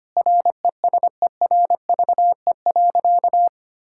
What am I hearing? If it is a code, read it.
Morse code, heard as a keyed tone, about 25 wpm, 700 Hz. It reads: RESERVE.
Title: Sentences from Top 1000 Words - Review Part 6 - 25wpm